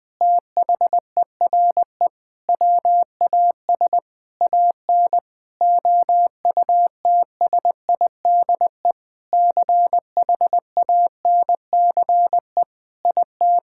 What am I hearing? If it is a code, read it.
Morse code, heard as a keyed tone, about 20 wpm, 700 Hz. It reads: THERE WAS AN OUTSIDE CHANCE IT
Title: Sentences from Top 1000 Words - Review Part 6 - 20wpm